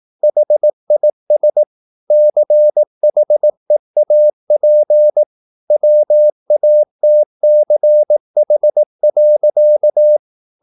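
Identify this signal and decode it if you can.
Morse code, heard as a keyed tone, about 18 wpm, 600 Hz. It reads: HIS CHEAP WATCH.
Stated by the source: Morse practice at 18 wpm, English text, from CW med Gnister.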